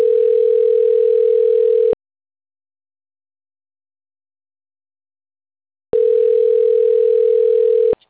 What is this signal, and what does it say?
Ringback tone (North America), 440 + 480 Hz.
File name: autotestagent / 11-41-30.136..callcentertransfer. .d1d1d6c4-aee8-418f-8da3